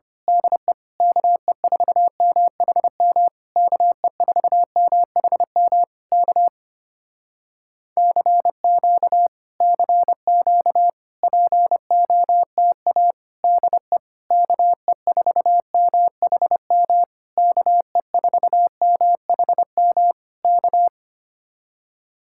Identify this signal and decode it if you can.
Morse code, heard as a keyed tone, about 24 wpm, 700 Hz. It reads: DE KE4MHM KE4MHM K CQ CQ POTA DE KE4MHM KE4MHM K
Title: Speed-Racing Common QSO Phrases Encore - 25wpm